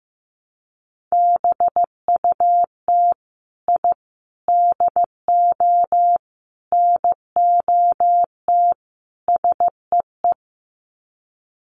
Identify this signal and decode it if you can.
Morse code, heard as a keyed tone, about 15 wpm, 700 Hz. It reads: BUT I DO NOT SEE
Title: Sets of 5 Words Encore 15wpm